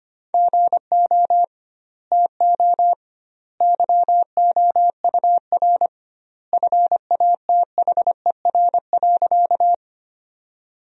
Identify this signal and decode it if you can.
Morse code, heard as a keyed tone, about 25 wpm, 700 Hz. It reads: GO TO YOUR FATHER.
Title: 2x Word-Spacing - Sentences from Top 300 Words - 25wpm